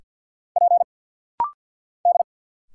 Morse code, heard as a keyed tone, about 50 wpm, 700 Hz. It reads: P D
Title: Single Letters - Rapid-Fire 50wpm